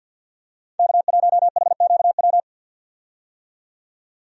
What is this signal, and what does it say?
Morse code, heard as a keyed tone, about 52 wpm, 700 Hz. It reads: K1HQW
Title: USA Call Signs - 5 Characters - 50wpm